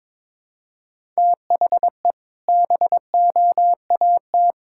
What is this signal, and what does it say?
Morse code, heard as a keyed tone, about 22 wpm, 700 Hz. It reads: THE BOAT
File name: practice / 22wpm / Sentences with 4-Word Phrases - 22wpm